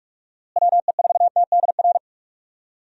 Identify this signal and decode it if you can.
Morse code, heard as a keyed tone, about 44 wpm, 700 Hz. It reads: WE4TDR
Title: USA Call Signs - 6 Characters - Repeated 3x - 45wpm